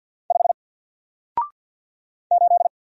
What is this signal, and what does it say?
Morse code, heard as a keyed tone, about 50 wpm, 700 Hz. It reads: 5 8
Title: Single Character (Letter:Number:Punctuation:Prosign) 50wpm